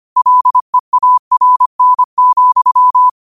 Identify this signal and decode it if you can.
Morse code, heard as a keyed tone, about 25 wpm, 1000 Hz. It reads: LEARN,